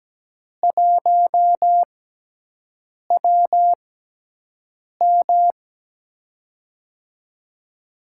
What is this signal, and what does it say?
Morse code, heard as a keyed tone, about 17 wpm, spaced as for 6 wpm, 700 Hz. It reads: 1WM
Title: USA Call Signs - 4 Characters - 6x Character Spacing - 17wpm